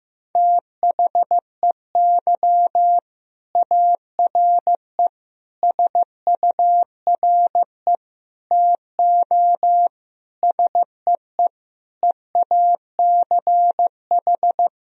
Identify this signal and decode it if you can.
Morse code, heard as a keyed tone, about 15 wpm, 700 Hz. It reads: THEY ARE SURE TO SEE EACH
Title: Sentences from Top 1000 Words - Review Part 1 - 15wpm